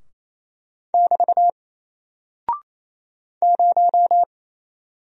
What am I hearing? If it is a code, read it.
Morse code, heard as a keyed tone, about 27 wpm, 700 Hz. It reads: <BT> 0
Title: Single Character (Letter:Number:Punctuation:Prosign) Alt 28wpm